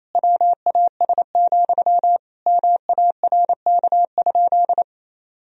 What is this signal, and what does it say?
Morse code, heard as a keyed tone, about 28 wpm, 700 Hz. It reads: WAS, MARK?
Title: Sentences from Top 1000 Words - Review Part 1 - 28wpm